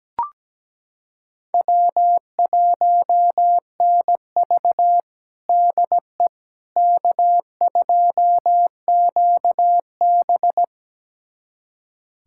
Morse code, heard as a keyed tone, about 17 wpm, 700 Hz. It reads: W1NV DE K2QB
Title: USA Callsign DE Callsign - 4 Characters - 17wpm